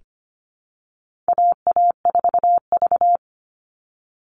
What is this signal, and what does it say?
Morse code, heard as a keyed tone, about 25 wpm, 700 Hz. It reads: AA4V
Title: Call Signs Repeated 2x - 25wpm